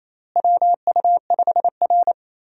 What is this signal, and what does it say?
Morse code, heard as a keyed tone, about 28 wpm, 700 Hz. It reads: WU5R
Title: USA Call Signs - 4 Characters - Repeated 3x - 28wpm